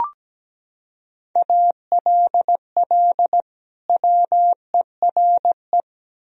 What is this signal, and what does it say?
Morse code, heard as a keyed tone, about 17 wpm, 700 Hz. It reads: ALL WERE
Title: Sentences from Top 1000 Words - Review Part 4 - 17wpm